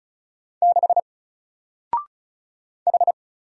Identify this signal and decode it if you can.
Morse code, heard as a keyed tone, about 37 wpm, 700 Hz. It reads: 6 H
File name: Single-Letter-Number-35wpm